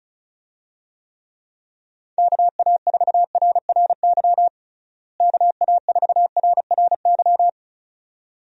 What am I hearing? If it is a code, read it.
Morse code, heard as a keyed tone, about 36 wpm, 700 Hz. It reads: KA4RRY KA4RRY
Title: USA Call Signs - 6 Characters - Repeated 2x - 35wpm